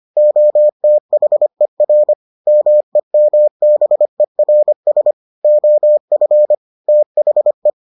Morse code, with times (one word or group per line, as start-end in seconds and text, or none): 0.2-2.1 OTHER
2.5-5.1 MEMBERS
5.4-6.5 OF
6.9-7.7 THE